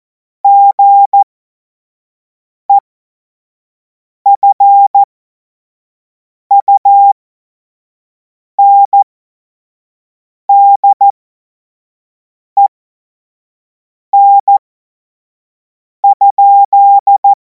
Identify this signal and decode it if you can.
Morse code, heard as a keyed tone, about 14 wpm, spaced as for 5 wpm, 800 Hz. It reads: GEFUNDEN?
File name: Morsekurs / Witze / Witz 25. August 2026 25 F70